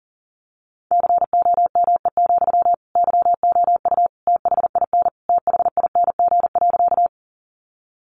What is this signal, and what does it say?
Morse code, heard as a keyed tone, about 40 wpm, 700 Hz. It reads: COME, YOU THIN THING.
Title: Sentences from Top 1200 Words 40wpm